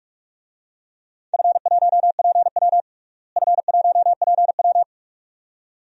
Morse code, heard as a keyed tone, about 41 wpm, 700 Hz. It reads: F1PW F1PW
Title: DXCC-100 Call Signs - 4 Characters - Repeated 2x - 45wpm